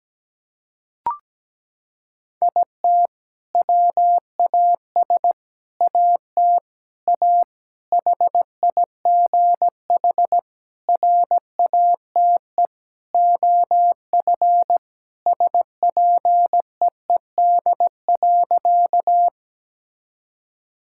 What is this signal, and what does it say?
Morse code, heard as a keyed tone, about 17 wpm, 700 Hz. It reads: IT WAS AT A HIGH RATE OF SPEED.
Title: Sentences from Top 1000 Words - Review Part 5 - 17wpm